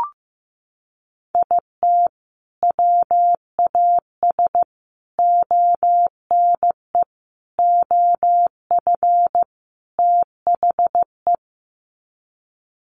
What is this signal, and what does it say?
Morse code, heard as a keyed tone, about 15 wpm, 700 Hz. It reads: IT WAS ONE OF THE
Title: Sets of 5 Words 15wpm